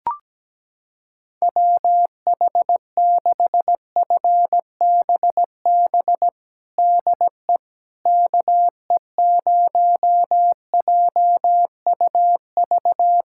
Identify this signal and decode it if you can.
Morse code, heard as a keyed tone, about 17 wpm, 700 Hz. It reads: WH6FBB DE KE0JUV
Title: USA Callsign DE Callsign 17wpm